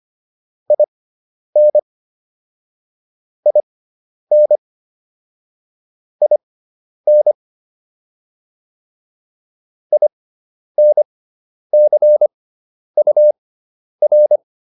Morse code, heard as a keyed tone, about 25 wpm, spaced as for 10 wpm, 600 Hz. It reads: IN IN IN INCUR